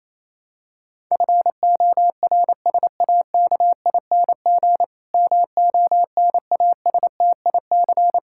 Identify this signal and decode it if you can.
Morse code, heard as a keyed tone, about 28 wpm, 700 Hz. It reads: FORSAKING MONASTIC
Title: Pangram Challenge v2 - 28wpm